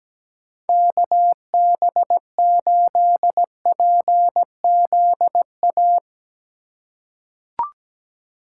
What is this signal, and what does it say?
Morse code, heard as a keyed tone, about 17 wpm, 700 Hz. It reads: KB8PZA
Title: USA Call Signs - 6 Characters - 3x Character Spacing - 17wpm